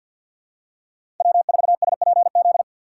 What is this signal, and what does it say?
Morse code, heard as a keyed tone, about 53 wpm, 700 Hz. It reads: W4SPZ
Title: USA Call Signs - 5 Characters - 2x Character Spacing - 50wpm